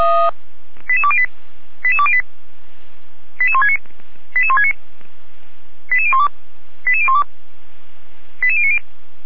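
Five-tone selective call (ZVEI), heard as quick runs of five tones, tones of about 70 ms. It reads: 88208 88208 88169 88169 88012 88012 88090
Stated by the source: ZVEI five-tone selective call 88352 received by SDR.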